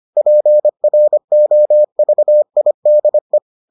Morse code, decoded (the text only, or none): PROVIDE